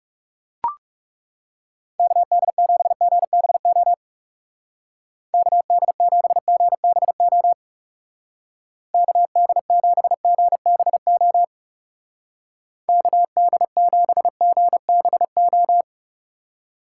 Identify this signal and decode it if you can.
Morse code, heard as a keyed tone, about 37 wpm, 700 Hz. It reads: KD7GBO KD7GBO KD7GBO KD7GBO
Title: Speed Racing - USA 2x3 Call Signs 30wpm